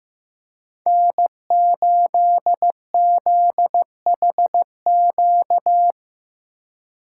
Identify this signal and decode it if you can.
Morse code, heard as a keyed tone, about 15 wpm, 700 Hz. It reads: N8ZHQ
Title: USA Call Signs - 5 Characters - 6x Character Spacing - 15wpm